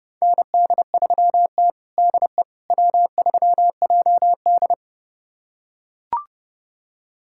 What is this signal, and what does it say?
Morse code, heard as a keyed tone, about 32 wpm, 700 Hz. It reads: ND3T DE W3JD
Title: USA Callsign DE Callsign - 4 Characters - 30wpm